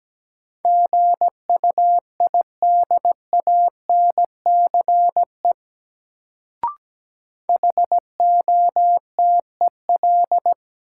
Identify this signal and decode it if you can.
Morse code, heard as a keyed tone, about 17 wpm, 700 Hz. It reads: GUIDANCE HOTEL
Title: Sentences from Top 1600 Words - New Word Drill Alt - 17wpm